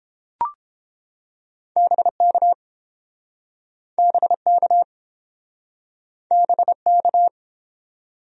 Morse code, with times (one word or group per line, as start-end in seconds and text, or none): 1.8-2.5 BK
4.0-4.8 BK
6.3-7.3 BK